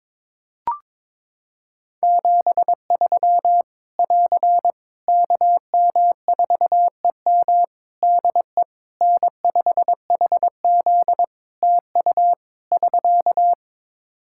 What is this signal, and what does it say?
Morse code, heard as a keyed tone, about 22 wpm, 700 Hz. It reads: 73 <AR> KM4EM DE N5HZ TU <SK>
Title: Practice QSO - Exchange 4 - Ending - 22wpm